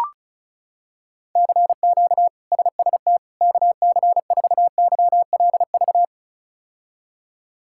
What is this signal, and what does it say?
Morse code, heard as a keyed tone, about 34 wpm, 700 Hz. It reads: CQ SST KC4YLV
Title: K1USN SST QSO Practice - 35wpm